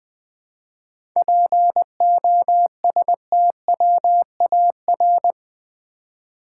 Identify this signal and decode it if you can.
Morse code, heard as a keyed tone, about 20 wpm, 700 Hz. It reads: POSTWAR
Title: 7 Letter Words - First 3 Letters Given - 20wpm